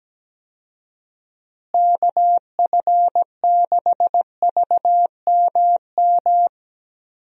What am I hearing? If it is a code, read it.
Morse code, heard as a keyed tone, about 17 wpm, 700 Hz. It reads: KF6VMM
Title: USA Call Signs - 6 Characters - Repeated 3x - 17wpm